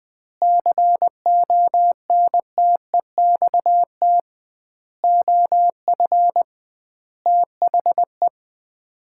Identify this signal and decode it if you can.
Morse code, heard as a keyed tone, about 20 wpm, 700 Hz. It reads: CONTEXT OF THE
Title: 2x Word-Spacing - Sets of 5 Words - 20wpm